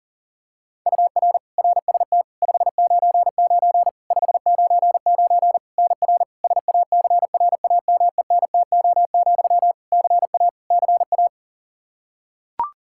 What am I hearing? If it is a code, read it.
Morse code, heard as a keyed tone, about 37 wpm, 700 Hz. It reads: UR RST 599 599 NR SACRAMENTO, CA CA
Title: K1USN SST QSO Practice 40wpm